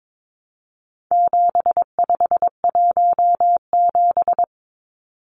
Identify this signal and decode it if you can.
Morse code, heard as a keyed tone, about 22 wpm, 700 Hz. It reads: 7517
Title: Numbers - 4-Digit 22wpm